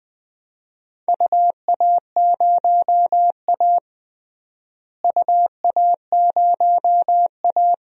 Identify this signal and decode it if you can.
Morse code, heard as a keyed tone, about 20 wpm, 700 Hz. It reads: UA0A UA0A
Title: DXCC-100 Call Signs - 4 Characters - Repeated 3x - 20wpm